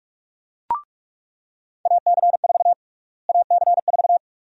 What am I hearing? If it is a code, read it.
Morse code, heard as a keyed tone, about 42 wpm, 700 Hz. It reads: AC4 AC4
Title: USA Call Sign Prefixes - Repeated 2x - 45wpm